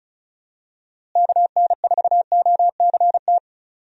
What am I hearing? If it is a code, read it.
Morse code, heard as a keyed tone, about 34 wpm, 700 Hz. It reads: KN4OCT
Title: USA Call Signs - 6 Characters - Repeated 3x - 35wpm